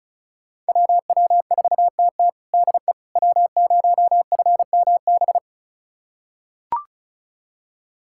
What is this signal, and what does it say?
Morse code, heard as a keyed tone, about 36 wpm, 700 Hz. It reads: WW4TT DE W0FMB
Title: USA Callsign DE Callsign - 5 Characters - 35wpm